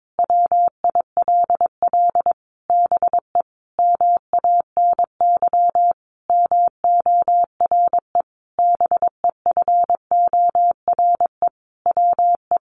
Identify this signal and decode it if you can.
Morse code, heard as a keyed tone, about 22 wpm, 700 Hz. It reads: WILL BE MANY MORE BEFORE WE
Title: Sentences from Top 1500 Words 22wpm